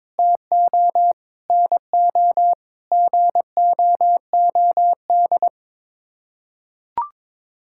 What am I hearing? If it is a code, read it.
Morse code, heard as a keyed tone, about 22 wpm, 700 Hz. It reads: TO NO GOOD
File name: Sets of 9 Words - 22wpm